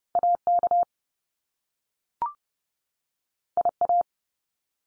Morse code, heard as a keyed tone, about 30 wpm, 700 Hz. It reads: AK IA
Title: US States 30WPM